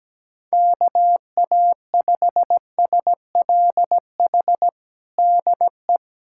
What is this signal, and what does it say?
Morse code, heard as a keyed tone, about 17 wpm, 700 Hz. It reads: KA5SLH DE
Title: USA Callsign DE Callsign 17wpm